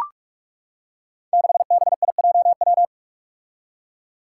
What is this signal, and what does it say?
Morse code, heard as a keyed tone, about 44 wpm, 700 Hz. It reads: 6BIJW
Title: Five Challenging Letter-Number Combinations 45wpm